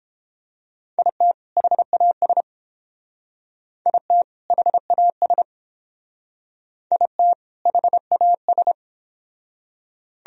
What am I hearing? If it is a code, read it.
Morse code, heard as a keyed tone, about 29 wpm, 700 Hz. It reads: IT HAS IT HAS IT HAS